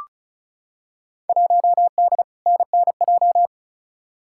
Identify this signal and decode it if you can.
Morse code, heard as a keyed tone, about 36 wpm, 700 Hz. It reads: 1D NNJ
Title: Field Day Exchanges Encore - 35wpm